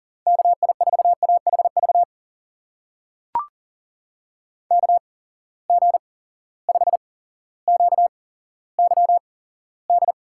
Morse code, heard as a keyed tone, about 40 wpm, spaced as for 17 wpm, 700 Hz. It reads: KI4AHV KG5QYD
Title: USA Call Signs - 6 Characters - 8x Character Spacing - 40wpm